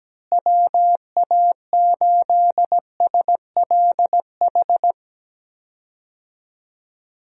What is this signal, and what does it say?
Morse code, heard as a keyed tone, about 17 wpm, 700 Hz. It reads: WA8SLH
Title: USA Call Signs - 6 Characters - Repeated 3x - 17wpm